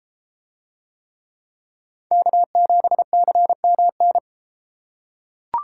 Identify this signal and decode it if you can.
Morse code, heard as a keyed tone, about 31 wpm, 700 Hz. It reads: K7CMN